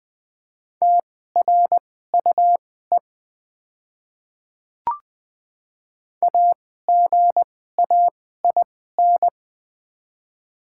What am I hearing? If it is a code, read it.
Morse code, heard as a keyed tone, about 20 wpm, spaced as for 14 wpm, 690 Hz. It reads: TRUE AGAIN
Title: 2x Character Spacing - Top 500 Words - 20wpm